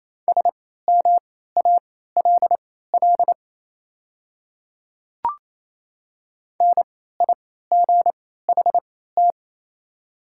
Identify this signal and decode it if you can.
Morse code, heard as a keyed tone, about 28 wpm, spaced as for 16 wpm, 700 Hz. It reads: SMALL NIGHT
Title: 3x Character Spacing - Top 300 Words - 28wpm